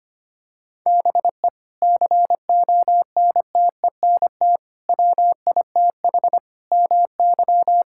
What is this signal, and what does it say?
Morse code, heard as a keyed tone, about 25 wpm, 700 Hz. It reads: BE CONTENT WITH MY
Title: Sentences from Top 1000 Words - Review Part 6 - 25wpm